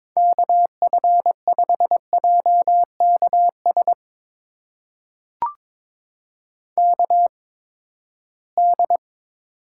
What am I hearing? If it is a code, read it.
Morse code, heard as a keyed tone, about 22 wpm, spaced as for 15 wpm, 700 Hz. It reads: KF5JKS KD
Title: USA Call Signs - 6 Characters - 8x Character Spacing - 22wpm